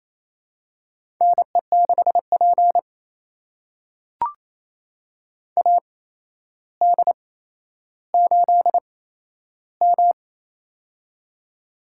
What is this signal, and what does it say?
Morse code, heard as a keyed tone, about 27 wpm, spaced as for 12 wpm, 700 Hz. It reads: NE6P AD8M
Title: USA Call Signs - 4 Characters - 8x Character Spacing - 28wpm